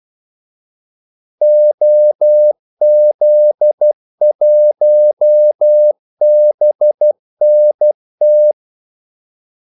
Morse code, heard as a keyed tone, about 12 wpm, 600 Hz. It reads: OZ1BNT
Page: 2 ekstra ordmellemrum mellem kaldesignalerne